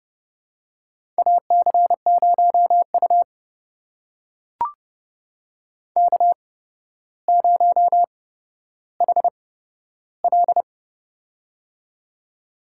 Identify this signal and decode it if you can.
Morse code, heard as a keyed tone, about 30 wpm, spaced as for 13 wpm, 700 Hz. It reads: AC0U K0HL
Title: USA Call Signs - 4 Characters - 8x Character Spacing - 30wpm